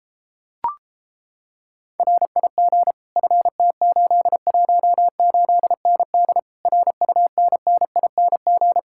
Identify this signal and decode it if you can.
Morse code, heard as a keyed tone, about 33 wpm, 700 Hz. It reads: RIG FT818ND RUNNING